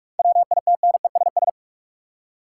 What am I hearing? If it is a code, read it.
Morse code, heard as a keyed tone, about 46 wpm, 700 Hz. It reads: WITNESS
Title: Seven Letter Words - 45wpm